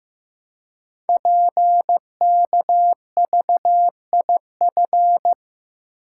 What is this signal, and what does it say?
Morse code, heard as a keyed tone, about 15 wpm, 700 Hz. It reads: PKVIF
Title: Code Groups - 5 Letters - 15wpm